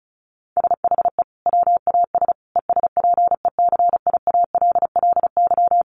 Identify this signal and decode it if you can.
Morse code, heard as a keyed tone, about 34 wpm, 700 Hz. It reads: SHE WAS ESPECIALLY
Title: Sentences from Top 700 Words 35wpm